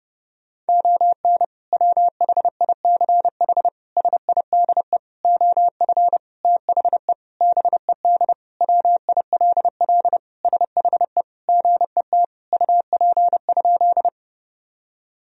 Morse code, heard as a keyed tone, about 32 wpm, 700 Hz. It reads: ON WHICH SIDE OF THE BED WILL SHE GET UP?
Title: Sentences from Top 1000 Words - Review Part 1 - 30wpm